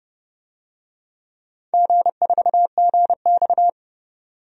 Morse code, heard as a keyed tone, about 30 wpm, 700 Hz. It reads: G4GX
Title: DXCC-100 Call Signs - 4 Characters - 30wpm